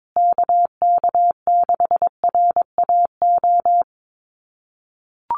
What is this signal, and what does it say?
Morse code, heard as a keyed tone, about 22 wpm, 700 Hz. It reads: KK6RAO
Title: USA Call Signs Repeated 3x 22wpm